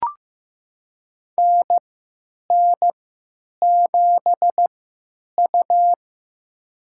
Morse code, heard as a keyed tone, about 15 wpm, spaced as for 9 wpm, 700 Hz. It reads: NN7U
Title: USA Call Signs - 4 Characters - 3x Character Spacing - 15wpm